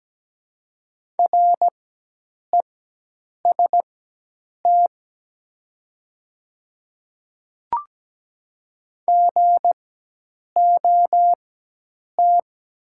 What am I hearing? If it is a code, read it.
Morse code, heard as a keyed tone, about 17 wpm, spaced as for 8 wpm, 700 Hz. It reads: REST GOT
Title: 4x Character Spacing - Top 500 Words - 17wpm